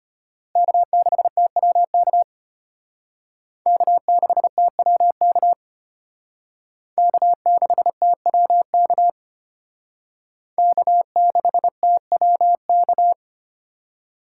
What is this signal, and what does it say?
Morse code, heard as a keyed tone, about 32 wpm, 700 Hz. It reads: K6TWK K6TWK K6TWK K6TWK